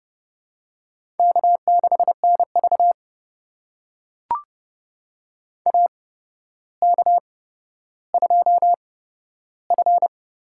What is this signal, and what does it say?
Morse code, heard as a keyed tone, about 31 wpm, spaced as for 13 wpm, 700 Hz. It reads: K6NV AK2F
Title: USA Call Signs - 4 Characters - 8x Character Spacing - 30wpm